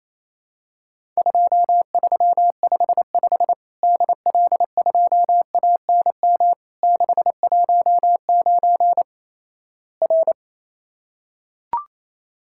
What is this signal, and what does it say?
Morse code, heard as a keyed tone, about 28 wpm, 700 Hz. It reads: 2355 DL2ANM 619 R
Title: Worked All Europe DX Contest - QTC Exchange - 28wpm